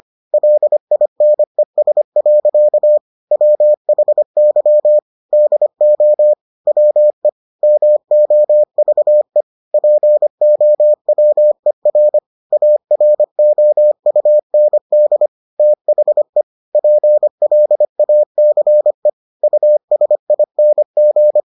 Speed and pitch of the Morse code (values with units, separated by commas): 25 wpm, 600 Hz